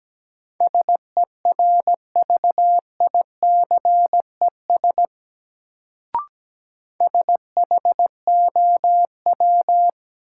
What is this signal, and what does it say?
Morse code, heard as a keyed tone, about 17 wpm, 700 Hz. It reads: SERVICES SHOW
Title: Sets of 5 Words - New Word Drill - 17wpm